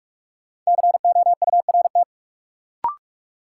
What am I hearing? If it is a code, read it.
Morse code, heard as a keyed tone, about 46 wpm, 700 Hz. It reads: COURT
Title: Sentences from Top 1600 Words - New Word Drill Alt - 45wpm